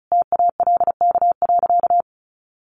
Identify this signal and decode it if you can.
Morse code, heard as a keyed tone, about 34 wpm, 700 Hz. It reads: TALK.
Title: Sentences from Top 600 Words 35wpm